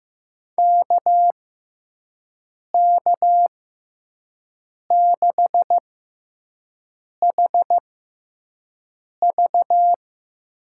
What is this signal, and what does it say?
Morse code, heard as a keyed tone, about 15 wpm, spaced as for 5 wpm, 700 Hz. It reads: KK6HV
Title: USA Call Signs - 6x Character Spacing - 15wpm